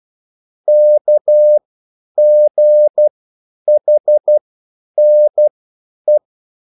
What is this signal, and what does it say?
Morse code, heard as a keyed tone, about 12 wpm, spaced as for 9 wpm, 600 Hz. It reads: KGHNE